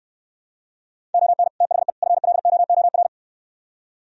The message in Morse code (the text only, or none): ON THE FLOOR